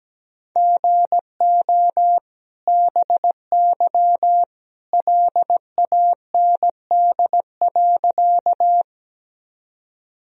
Morse code, heard as a keyed tone, about 17 wpm, 700 Hz. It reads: GO BY LAND.
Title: Sentences from Top 1000 Words - Review Part 1 - 17wpm